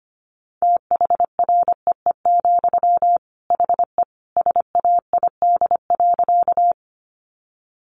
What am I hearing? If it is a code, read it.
Morse code, heard as a keyed tone, about 25 wpm, 700 Hz. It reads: THREE, HE SAID.
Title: Sentences from Top 200 Words 25wpm